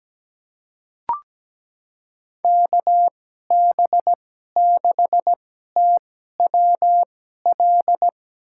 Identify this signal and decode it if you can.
Morse code, heard as a keyed tone, about 17 wpm, spaced as for 12 wpm, 700 Hz. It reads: KB6TWL
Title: USA Call Signs - 6 Characters - 2x Character Spacing - 17wpm